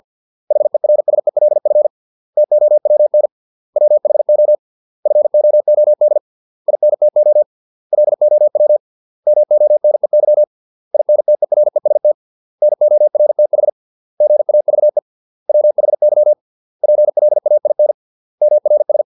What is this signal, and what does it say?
Morse code, heard as a keyed tone, about 50 wpm, 600 Hz. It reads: HERSELF TOWN WHO FOOD INTO LOW MONEY INTEREST NORTH GAVE WHY PLAIN MRS